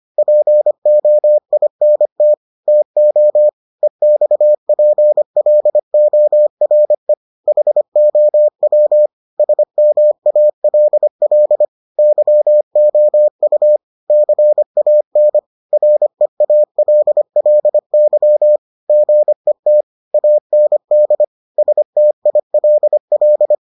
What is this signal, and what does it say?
Morse code, heard as a keyed tone, about 25 wpm, 600 Hz. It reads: POINT TO EXPLORE HOW SMALL YOU CAN REALLY GET AND STILL